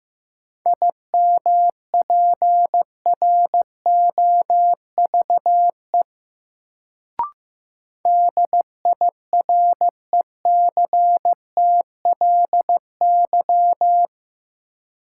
Morse code, heard as a keyed tone, about 15 wpm, 700 Hz. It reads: IMPROVE DIRECTLY
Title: Sentences from Top 800 Words - New Word Drill - 15wpm